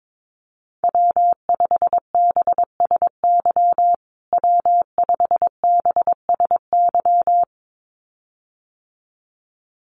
Morse code, heard as a keyed tone, about 22 wpm, 700 Hz. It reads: W5BSY W5BSY
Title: USA Call Signs Repeated 2x 22wpm